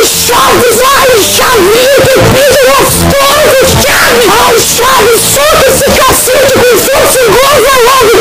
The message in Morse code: none